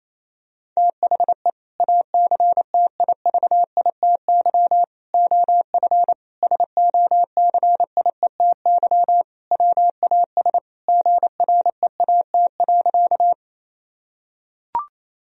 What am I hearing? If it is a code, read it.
Morse code, heard as a keyed tone, about 28 wpm, 700 Hz. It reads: THE ACTIVITY OF SOCIETY WAS GREAT.
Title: Sentences from Top 1000 Words - Review Part 4 - 28wpm